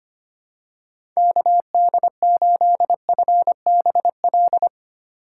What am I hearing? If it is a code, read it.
Morse code, heard as a keyed tone, about 25 wpm, 700 Hz. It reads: KD8FBL
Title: USA Call Signs - 6 Characters - 4x Character Spacing - 25wpm